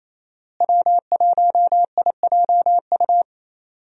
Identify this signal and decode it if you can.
Morse code, heard as a keyed tone, about 28 wpm, 700 Hz. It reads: W1IJU
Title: USA Call Signs - 5 Characters - Repeated 3x - 28wpm